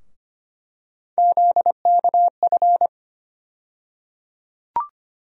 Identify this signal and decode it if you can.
Morse code, heard as a keyed tone, about 25 wpm, 700 Hz. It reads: ZKF
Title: USA Call Sign Postfixes - 3 Letters 25wpm